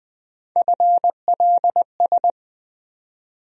Morse code, heard as a keyed tone, about 20 wpm, 700 Hz. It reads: FLS
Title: USA Call Sign Postfixes - 3 Letters 20wpm